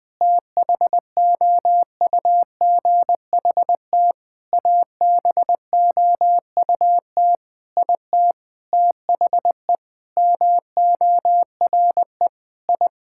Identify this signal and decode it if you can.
Morse code, heard as a keyed tone, about 20 wpm, 700 Hz. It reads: THOUGHT ABOUT IT THE MORE I
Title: Sentences with 8-Word Phrases - 20wpm